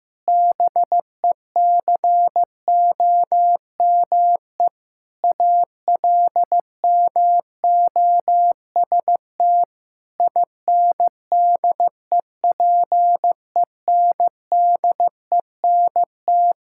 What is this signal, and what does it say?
Morse code, heard as a keyed tone, about 16 wpm, 700 Hz. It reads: BECOME ALMOST INDEPENDENT
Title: Sentences from Top 1000 Words - Review Part 6 - 15wpm